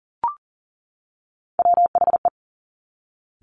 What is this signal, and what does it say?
Morse code, heard as a keyed tone, about 39 wpm, 700 Hz. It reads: WHE
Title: Top Three Letter Combinations in Words 40wpm